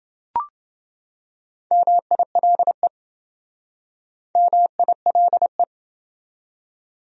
Morse code, heard as a keyed tone, about 27 wpm, 700 Hz. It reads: MILE MILE